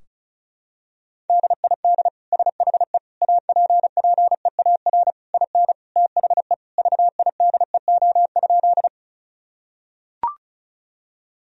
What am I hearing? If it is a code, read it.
Morse code, heard as a keyed tone, about 36 wpm, 700 Hz. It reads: DID SHE APPEAR IN THE VIDEO?
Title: Sentences from Top 1000 Words - Review Part 4 - 35wpm